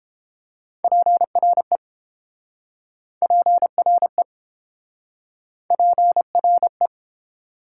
Morse code, heard as a keyed tone, about 30 wpm, 700 Hz. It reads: PRE PRE PRE